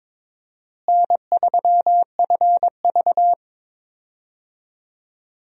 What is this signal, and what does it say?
Morse code, heard as a keyed tone, about 22 wpm, 700 Hz. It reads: N3FV
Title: USA Call Signs - 4 Characters - Repeated 3x - 22wpm